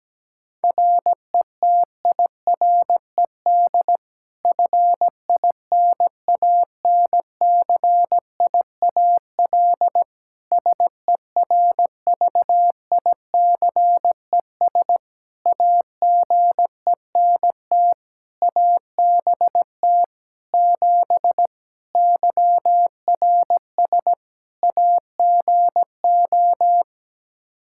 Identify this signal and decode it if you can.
Morse code, heard as a keyed tone, about 17 wpm, 700 Hz. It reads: RETIRED FINANCIAL SERVICES AGENT ABT 7 YRS AGO
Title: K1USN SST QSO Practice 17wpm